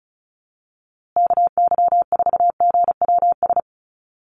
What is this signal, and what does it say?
Morse code, heard as a keyed tone, about 34 wpm, 700 Hz. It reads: KY4GWS
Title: USA Call Signs Repeated 3x 35wpm